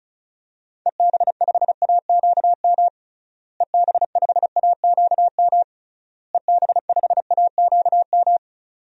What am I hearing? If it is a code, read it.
Morse code, heard as a keyed tone, about 36 wpm, 700 Hz. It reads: EB5AQM EB5AQM EB5AQM
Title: DXCC-100 Call Signs - 6 Characters - Repeated 3x - 35wpm